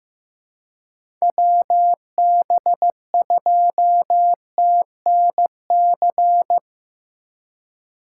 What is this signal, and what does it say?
Morse code, heard as a keyed tone, about 15 wpm, 700 Hz. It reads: WB2TNC
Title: USA Call Signs - 6 Characters - 4x Character Spacing - 15wpm